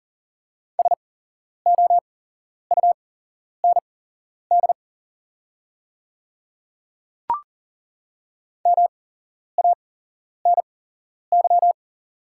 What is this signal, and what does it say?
Morse code, heard as a keyed tone, about 40 wpm, spaced as for 11 wpm, 700 Hz. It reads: SOUND MANY
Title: 8x Character Spacing - Top 100 Words - 40wpm